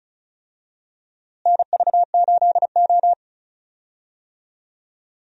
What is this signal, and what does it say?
Morse code, heard as a keyed tone, about 36 wpm, 700 Hz. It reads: NV8O
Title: USA Call Signs - 4 Characters - Repeated 3x - 35wpm